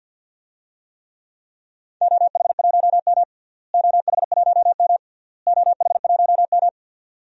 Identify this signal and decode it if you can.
Morse code, heard as a keyed tone, about 51 wpm, 700 Hz. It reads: OH1M OH1M OH1M
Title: DXCC-100 Call Signs - 4 Characters - Repeated 3x - 50wpm